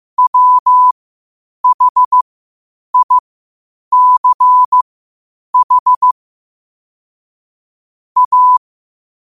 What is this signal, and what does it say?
Morse code, heard as a keyed tone, about 15 wpm, spaced as for 8 wpm, 1000 Hz. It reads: WHICH A
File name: qotdfiles / 8wpm